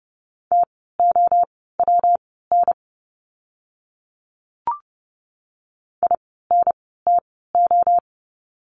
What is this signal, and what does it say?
Morse code, heard as a keyed tone, about 30 wpm, spaced as for 17 wpm, 700 Hz. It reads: TOWN INTO
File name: course / 30:10wpm / 115 Lesson 19f - Letter w - Words 30wpm